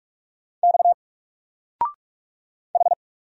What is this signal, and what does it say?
Morse code, heard as a keyed tone, about 44 wpm, 700 Hz. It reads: X H
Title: Single Letter-Number 45wpm